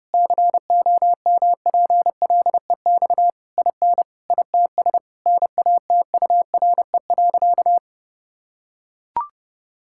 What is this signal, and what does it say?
Morse code, heard as a keyed tone, about 32 wpm, 700 Hz. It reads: COMPLEX IN ITS NATURE.
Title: Sentences from Top 1000 Words - Review Part 6 - 30wpm